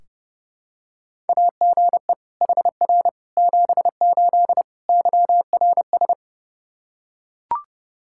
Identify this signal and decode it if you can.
Morse code, heard as a keyed tone, about 30 wpm, 700 Hz. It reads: AGE HR 78 YRS
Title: Practice QSO - Exchange 3 - More Info - 30wpm